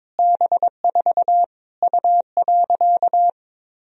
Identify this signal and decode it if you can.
Morse code, heard as a keyed tone, about 22 wpm, 700 Hz. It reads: B4 U.
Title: Sentences from Top 200 Words and Abbreviations 22wpm